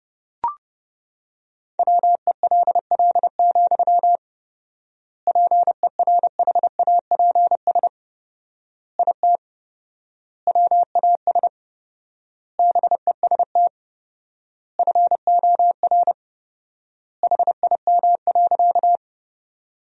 Morse code, heard as a keyed tone, about 32 wpm, 700 Hz. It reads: WELL, PERHAPS IT WAS BEST FOR HIM.
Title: 4x Word-Spacing - Sentences from Top 500 Words - 30wpm